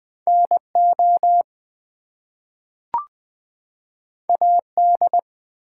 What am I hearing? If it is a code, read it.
Morse code, heard as a keyed tone, about 20 wpm, 700 Hz. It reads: NO AD
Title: Two Letter Words - 20wpm